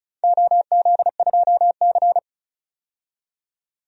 Morse code, heard as a keyed tone, about 36 wpm, 700 Hz. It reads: OZ2C
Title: DXCC-100 Call Signs - 4 Characters - 35wpm